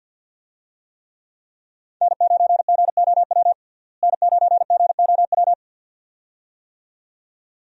Morse code, heard as a keyed tone, about 49 wpm, 700 Hz. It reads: N9GOW N9GOW
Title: USA Call Signs - 5 Characters - Repeated 2x - 50wpm